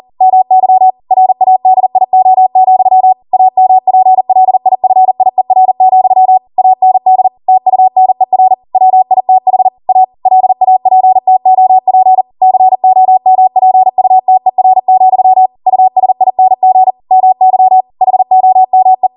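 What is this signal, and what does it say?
Morse code, heard as a keyed tone, about 40 wpm, 750 Hz. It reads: MY RADIO, AMPLIFIER, AND TUNER WITH A LAPTOP COMPUTER, USING MY HOME
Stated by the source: Code practice at 40 wpm.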